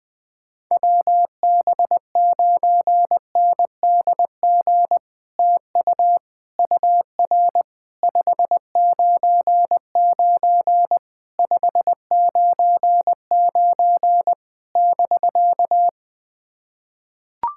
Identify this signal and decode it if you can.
Morse code, heard as a keyed tone, about 20 wpm, 700 Hz. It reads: WB9NDG TU UR 599 599 <BK>
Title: POTA QSO Practice - 20wpm